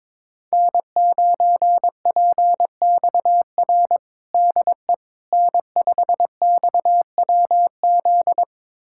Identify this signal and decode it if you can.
Morse code, heard as a keyed tone, about 22 wpm, 700 Hz. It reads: N9PXR DE N5XWZ
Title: USA Callsign DE Callsign - 5 Characters - 22wpm